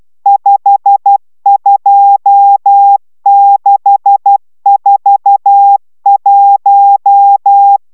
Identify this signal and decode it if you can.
Morse code, heard as a keyed tone, about 12 wpm, 800 Hz. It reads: 52641